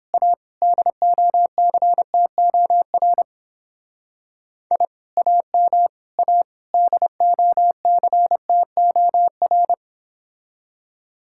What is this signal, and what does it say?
Morse code, heard as a keyed tone, about 29 wpm, 700 Hz. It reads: A DOCTOR I AM A DOCTOR